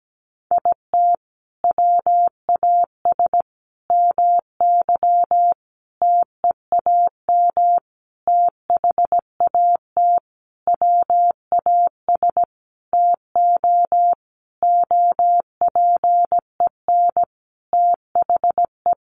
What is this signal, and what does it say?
Morse code, heard as a keyed tone, about 17 wpm, 700 Hz. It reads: IT WAS MY TEAM THAT WAS TO OPEN THE
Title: Sentences from Top 1500 Words 17wpm